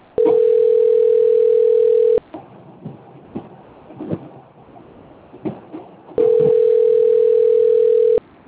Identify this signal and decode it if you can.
Ringback tone (North America), 440 + 480 Hz.